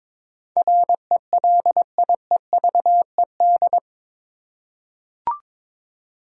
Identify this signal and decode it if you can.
Morse code, heard as a keyed tone, about 21 wpm, 700 Hz. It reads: RELIEVED
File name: practice / 22wpm / Eight Letter Words - 22wpm